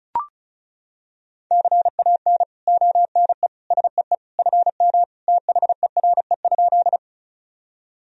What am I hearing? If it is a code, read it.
Morse code, heard as a keyed tone, about 36 wpm, 700 Hz. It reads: CAN ONE SEE FM THERE?
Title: Sentences from Top 100 Words and Abbreviations 35wpm